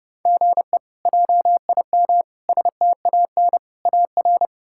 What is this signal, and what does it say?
Morse code, heard as a keyed tone, about 30 wpm, 700 Hz. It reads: GE JIM STAN AR
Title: K1USN SST QSO Practice - 30wpm